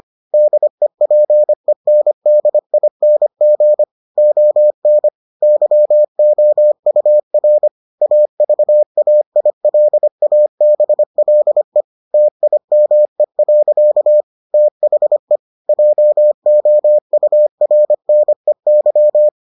Morse code, heard as a keyed tone, about 25 wpm, 600 Hz. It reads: DEPENDING ON YOUR AVAILABLE TIME. THE JOURNEY